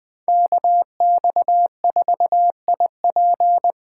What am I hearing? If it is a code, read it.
Morse code, heard as a keyed tone, about 20 wpm, 700 Hz. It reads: KX4IP
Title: Speed Racing - USA 2x2 Call Signs 17wpm